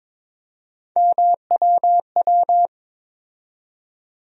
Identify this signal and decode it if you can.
Morse code, heard as a keyed tone, about 22 wpm, 700 Hz. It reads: MWW
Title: USA Call Sign Postfixes - 3 Letters 22wpm